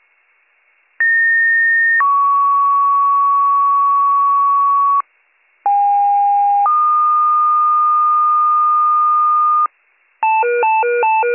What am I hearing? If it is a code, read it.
Two-tone page, heard as a one-second tone, then a three-second tone, about 1850 then 1120 Hz, about 810 then 1220 Hz. It is heard from 1.0 to 9.7 s.